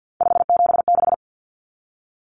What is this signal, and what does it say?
Morse code, heard as a keyed tone, about 49 wpm, 700 Hz. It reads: 576
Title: Numbers - 3-Digit 50wpm